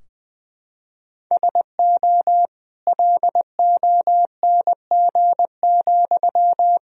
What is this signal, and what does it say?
Morse code, heard as a keyed tone, about 20 wpm, 700 Hz. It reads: SO LONG,
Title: Sentences from Top 1000 Words - Review Part 4 - 20wpm